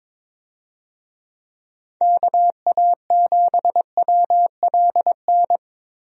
Morse code, heard as a keyed tone, about 22 wpm, 700 Hz. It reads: KA7WLN
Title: USA Call Signs - 6 Characters - 22wpm